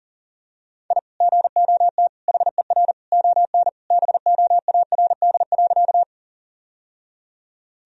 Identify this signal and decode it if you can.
Morse code, heard as a keyed tone, about 40 wpm, 700 Hz. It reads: I GOT HER ON BOARD.
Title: Sentences from Top 1000 Words - Review Part 4 - 40wpm